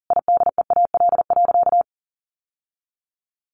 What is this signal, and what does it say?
Morse code, heard as a keyed tone, about 40 wpm, 700 Hz. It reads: IDEAL.
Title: Sentences from Top 1400 Words 40wpm